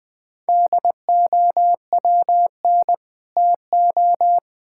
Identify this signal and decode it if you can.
Morse code, heard as a keyed tone, about 20 wpm, 700 Hz. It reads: DOWN TO
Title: Build A Phrase - 3-Word Phrases - 20wpm